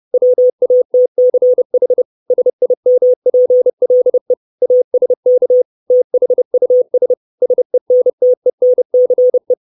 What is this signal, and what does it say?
Morse code, heard as a keyed tone, about 30 wpm, 500 Hz. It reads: WATCH SIMPLE ASK THUS SENTENCE
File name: 632 Words 30WPM